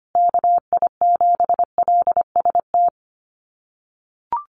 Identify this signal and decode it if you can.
Morse code, heard as a keyed tone, about 25 wpm, 700 Hz. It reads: KI7LST
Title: USA Call Signs Repeated 2x 25wpm